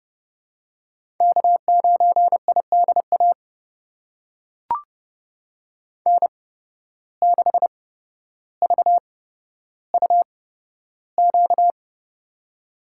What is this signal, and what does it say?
Morse code, heard as a keyed tone, about 30 wpm, spaced as for 13 wpm, 700 Hz. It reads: K9IDA N6VUQ
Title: USA Call Signs - 5 Characters - 8x Character Spacing - 30wpm